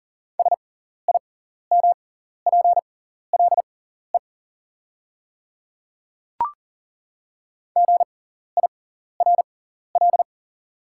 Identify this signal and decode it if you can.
Morse code, heard as a keyed tone, about 40 wpm, spaced as for 14 wpm, 690 Hz. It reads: SIMPLE GIRL
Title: 6x Character Spacing - Top 500 Words - 40wpm